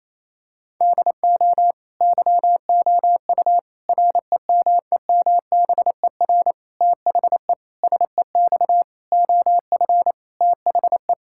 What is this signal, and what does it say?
Morse code, heard as a keyed tone, about 28 wpm, 700 Hz. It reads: DO YOU REMEMBER THE SEX OF THE
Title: Sentences from Top 1600 Words 28wpm